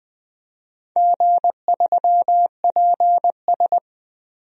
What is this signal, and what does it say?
Morse code, heard as a keyed tone, about 21 wpm, 700 Hz. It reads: G3PS
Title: DXCC-100 Call Signs - 4 Characters - 20wpm